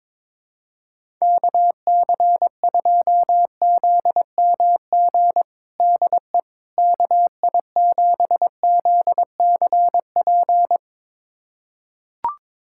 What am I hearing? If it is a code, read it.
Morse code, heard as a keyed tone, about 22 wpm, 700 Hz. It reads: KC2ZMG DE KI7ZCP
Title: USA Callsign DE Callsign - 6 Characters - 22wpm